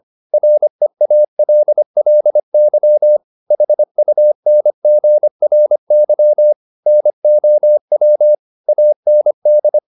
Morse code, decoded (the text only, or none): REALLY HUNGRY NOW AND